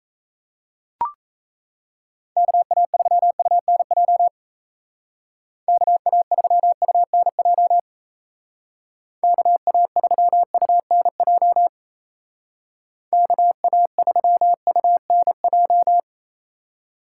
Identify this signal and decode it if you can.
Morse code, heard as a keyed tone, about 36 wpm, 700 Hz. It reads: KA3UNJ KA3UNJ KA3UNJ KA3UNJ